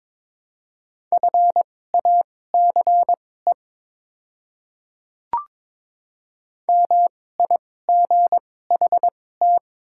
Morse code, heard as a keyed tone, about 22 wpm, spaced as for 16 wpm, 700 Hz. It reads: FACE MIGHT
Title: 2x Character Spacing - Top 500 Words - 22wpm